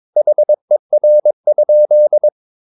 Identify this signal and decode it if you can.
Morse code, heard as a keyed tone, about 22 wpm, 600 Hz. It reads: HER?
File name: Ferie QRS 22wpm